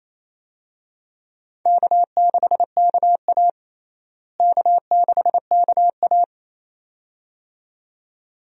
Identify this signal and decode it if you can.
Morse code, heard as a keyed tone, about 27 wpm, 700 Hz. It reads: K6KA K6KA
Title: USA Call Signs - 4 Characters - Repeated 2x - 28wpm